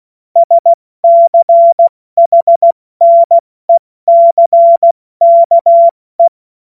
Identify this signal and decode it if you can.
Morse code, heard as a keyed tone, about 16 wpm, 680 Hz. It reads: SCHNECKE